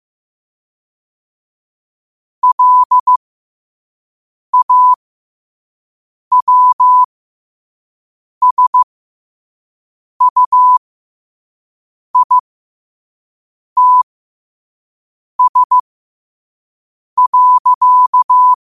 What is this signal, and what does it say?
Morse code, heard as a keyed tone, about 15 wpm, spaced as for 5 wpm, 1000 Hz. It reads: LAWSUITS.
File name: qotdfiles / 5wpm